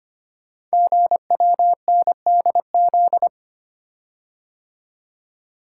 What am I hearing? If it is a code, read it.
Morse code, heard as a keyed tone, about 25 wpm, 700 Hz. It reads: GWNDZ
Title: Code Groups - 5 Letters - 25wpm